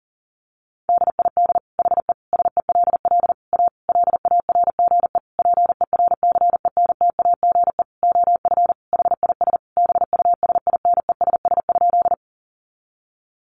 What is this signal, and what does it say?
Morse code, heard as a keyed tone, about 40 wpm, 700 Hz. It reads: DID HE SELL A LARGE PERCENTAGE OF HIS BUSINESS?
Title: Sentences from Top 1300 Words 40wpm